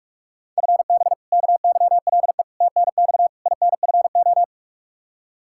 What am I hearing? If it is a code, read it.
Morse code, heard as a keyed tone, about 41 wpm, 700 Hz. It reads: FB KYLE TNX INFO
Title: Practice QSO - Exchange 3 - More Info - 45wpm